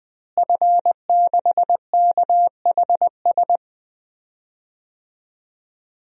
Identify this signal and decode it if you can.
Morse code, heard as a keyed tone, about 21 wpm, 700 Hz. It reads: F6KHS
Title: DXCC-100 Call Signs - 5 Characters - Repeated 2x - 20wpm